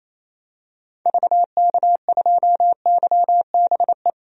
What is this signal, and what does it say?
Morse code, heard as a keyed tone, about 28 wpm, 700 Hz. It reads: VK2YBE
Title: DXCC-100 Call Signs - 6 Characters - 28wpm